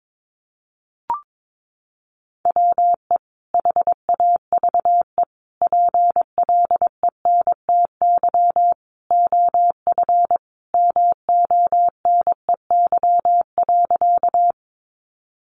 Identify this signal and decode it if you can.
Morse code, heard as a keyed tone, about 22 wpm, 700 Hz. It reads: WE HAVE PLENTY OF MONEY.
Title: Sentences from Top 1300 Words 22wpm